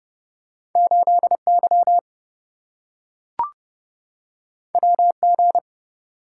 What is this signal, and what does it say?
Morse code, heard as a keyed tone, about 27 wpm, 700 Hz. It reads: 8Y WG